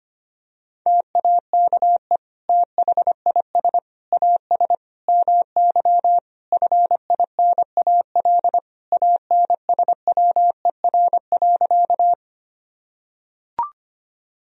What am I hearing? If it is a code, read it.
Morse code, heard as a keyed tone, about 25 wpm, 700 Hz. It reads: TAKE THIS AS MY FINAL ANSWER.
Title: Sentences from Top 1000 Words - Review Part 2 - 25wpm